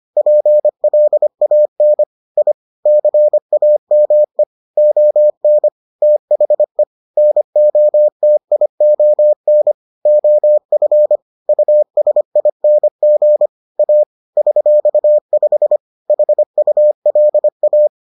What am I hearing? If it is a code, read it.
Morse code, heard as a keyed tone, about 25 wpm, 600 Hz. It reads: PLAN I CAME ON THE NOTION OF USING A $5 HULA